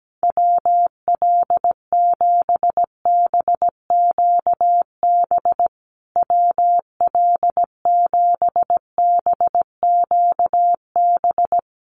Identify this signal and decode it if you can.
Morse code, heard as a keyed tone, about 17 wpm, 700 Hz. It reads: WL7BQB WL7BQB
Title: USA Call Signs Repeated 3x 17wpm